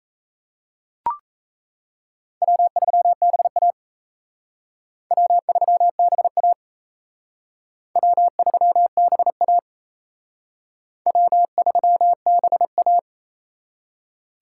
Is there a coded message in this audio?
Yes: Morse code from 2.4 to 13.0 s, heard as a keyed tone, about 36 wpm, 700 Hz. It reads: W3BA W3BA W3BA W3BA